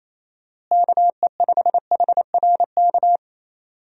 Morse code, heard as a keyed tone, about 28 wpm, 700 Hz. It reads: KE5HRK
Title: USA Call Signs - 6 Characters - Repeated 2x - 28wpm